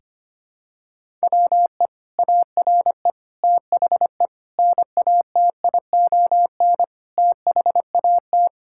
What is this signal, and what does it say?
Morse code, heard as a keyed tone, about 25 wpm, 700 Hz. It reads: WE ARE THE NATION THAT
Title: Sentences from Top 1000 Words - Review Part 6 - 25wpm